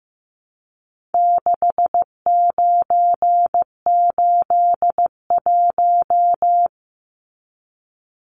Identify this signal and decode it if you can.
Morse code, heard as a keyed tone, about 15 wpm, 700 Hz. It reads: 6981
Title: Numbers - 4-Digit 15wpm